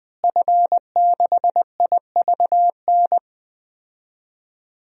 Morse code, heard as a keyed tone, about 20 wpm, 700 Hz. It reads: F6IVN
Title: DXCC-100 Call Signs - 5 Characters - Repeated 3x - 20wpm